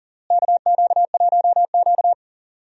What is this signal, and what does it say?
Morse code, heard as a keyed tone, about 40 wpm, 700 Hz. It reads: KQ1Q
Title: USA Call Signs - 4 Characters - Repeated 3x - 40wpm